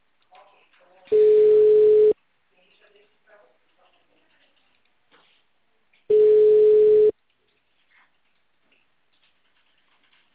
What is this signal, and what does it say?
Ringback tone (Europe, 425 Hz), with speech.
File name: out- -2002- - - .5416